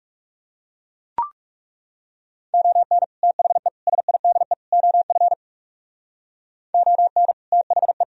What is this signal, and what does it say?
Morse code, heard as a keyed tone, about 35 wpm, 700 Hz. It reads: ON THE SIDE OF ON THE
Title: Speed-Racing - Sets of 4 Words 30wpm